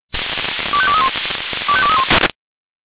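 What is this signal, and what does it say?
Five-tone selective call (ZVEI), heard as quick runs of five tones, tones of about 70 ms. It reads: 25231 25231
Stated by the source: ZVEI five-tone call 25212 received on SDR.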